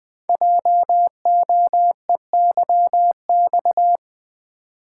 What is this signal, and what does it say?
Morse code, heard as a keyed tone, about 20 wpm, 700 Hz. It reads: JOEYX
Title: Code Groups - 5 Letter:Numbers - 20wpm